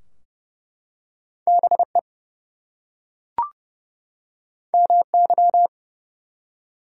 Morse code, heard as a keyed tone, about 29 wpm, 700 Hz. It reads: BE MY
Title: Two Letter Words - 30wpm